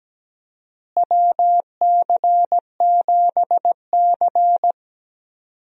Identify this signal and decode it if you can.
Morse code, heard as a keyed tone, about 17 wpm, 700 Hz. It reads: WC7C
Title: USA Call Signs - 4 Characters - Repeated 3x - 17wpm